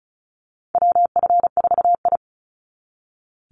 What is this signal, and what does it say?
Morse code, heard as a keyed tone, about 34 wpm, 700 Hz. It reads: WF4I
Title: NA SOTA Chaser Callsigns 35wpm